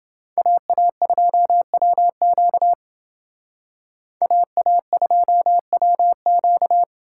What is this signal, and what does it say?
Morse code, heard as a keyed tone, about 28 wpm, 700 Hz. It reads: AA2WQ AA2WQ